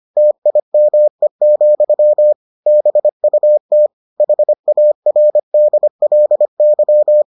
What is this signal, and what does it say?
Morse code, heard as a keyed tone, about 25 wpm, 600 Hz. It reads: TIME, BUT HARDLY